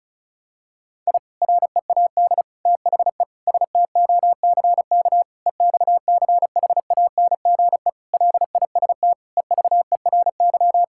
Morse code, heard as a keyed tone, about 36 wpm, 700 Hz. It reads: I READ THE STOCK EXCHANGE LIST EVERY
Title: Sentences from Top 1000 Words - Review Part 6 - 35wpm